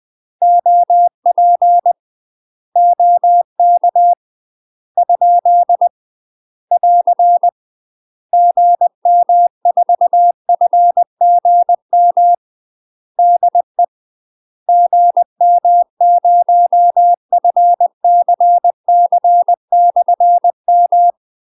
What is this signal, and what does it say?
Morse code, heard as a keyed tone, about 20 wpm, 700 Hz. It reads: OP OK ? <AR> GM4FGM DE GM0FCC/M
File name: UK MASTER QSOs TRAINING / 20WPM / UK MASTER QSOs 0013